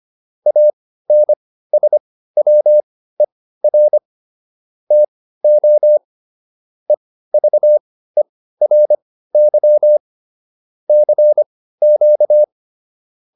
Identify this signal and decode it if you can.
Morse code, heard as a keyed tone, about 25 wpm, spaced as for 15 wpm, 600 Hz. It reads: ANSWER TO EVERY CQ